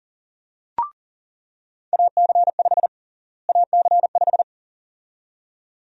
Morse code, heard as a keyed tone, about 40 wpm, 700 Hz. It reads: AC5 AC5
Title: USA Call Sign Prefixes - Repeated 2x - 40wpm